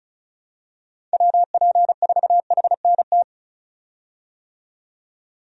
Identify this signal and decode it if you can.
Morse code, heard as a keyed tone, about 36 wpm, 700 Hz. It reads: WP4HNT
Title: USA Call Signs - 6 Characters - 35wpm